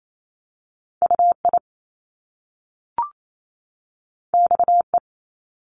Morse code, heard as a keyed tone, about 28 wpm, 700 Hz. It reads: UI XE
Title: Top Two Letter Combinations in Words 28wpm